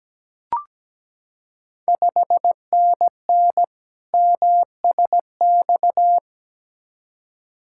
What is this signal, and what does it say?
Morse code, heard as a keyed tone, about 17 wpm, 690 Hz. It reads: 5NN MSX
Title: Ontario QSO Party Exchange 17wpm